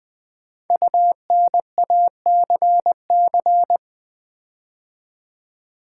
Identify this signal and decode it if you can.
Morse code, heard as a keyed tone, about 20 wpm, 700 Hz. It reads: UNACC
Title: Build A Word - 5 Letters At A Time - 12-Letter Words - 20wpm